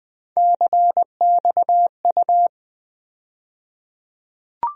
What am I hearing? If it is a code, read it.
Morse code, heard as a keyed tone, about 20 wpm, 700 Hz. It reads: CXU
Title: USA Call Sign Postfixes - 3 Letters 20wpm